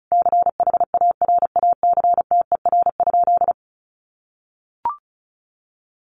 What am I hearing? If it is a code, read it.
Morse code, heard as a keyed tone, about 34 wpm, 700 Hz. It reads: CHARACTER?
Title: Sentences from Top 1200 Words 35wpm